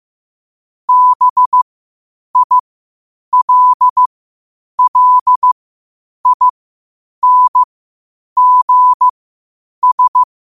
Morse code, heard as a keyed tone, about 15 wpm, spaced as for 8 wpm, 1000 Hz. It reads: BILLINGS
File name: qotdfiles / 8wpm